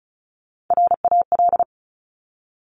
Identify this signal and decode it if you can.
Morse code, heard as a keyed tone, about 36 wpm, 700 Hz. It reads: RAL
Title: Top Three Letter Combinations in Words 35wpm